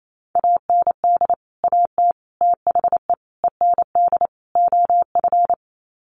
Morse code, heard as a keyed tone, about 28 wpm, 700 Hz. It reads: AND AT THE END OF
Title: Sets of 5 Words 28wpm